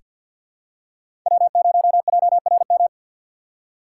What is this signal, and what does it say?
Morse code, heard as a keyed tone, about 48 wpm, 700 Hz. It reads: W0JRM
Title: USA Call Signs - 5 Characters - 2x Character Spacing - 50wpm